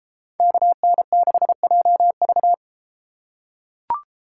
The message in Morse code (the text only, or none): KN6JV